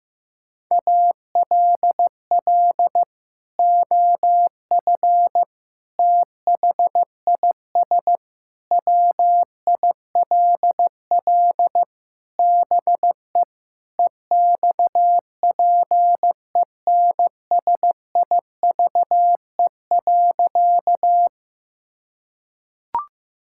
Morse code, heard as a keyed tone, about 16 wpm, 700 Hz. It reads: ALL OF THIS WILL BE EXPENSIVE.
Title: Sentences from Top 1000 Words - Review Part 6 - 15wpm